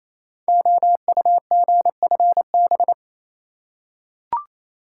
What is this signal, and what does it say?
Morse code, heard as a keyed tone, about 27 wpm, 700 Hz. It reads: OUGFB
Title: Code Groups - 5 Letters - 28wpm